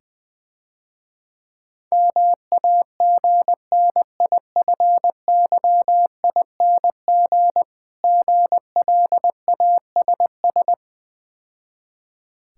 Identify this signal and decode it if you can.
Morse code, heard as a keyed tone, about 22 wpm, 690 Hz. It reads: MAGNIFYING GLASS
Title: Difficult Words - Speed-Racing with Hints - 20wpm